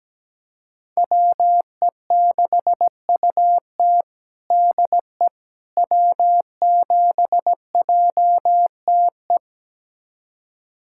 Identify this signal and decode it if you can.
Morse code, heard as a keyed tone, about 17 wpm, 700 Hz. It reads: WE6UT DE W7JTE
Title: USA Callsign DE Callsign - 5 Characters - 17wpm